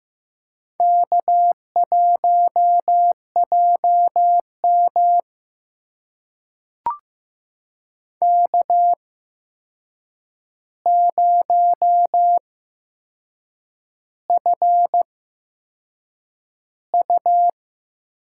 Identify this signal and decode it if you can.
Morse code, heard as a keyed tone, about 15 wpm, spaced as for 6 wpm, 700 Hz. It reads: K1JM K0FU
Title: USA Call Signs - 4 Characters - 8x Character Spacing - 15wpm